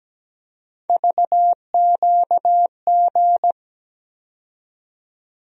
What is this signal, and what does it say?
Morse code, heard as a keyed tone, about 17 wpm, 700 Hz. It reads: VQG
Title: USA Call Sign Postfixes - 3 Letters 17wpm